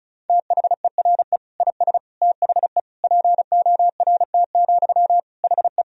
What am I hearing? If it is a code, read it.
Morse code, heard as a keyed tone, about 36 wpm, 700 Hz. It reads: THERE IS THE PORT, HE
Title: Sentences from Top 1000 Words - Review Part 1 - 35wpm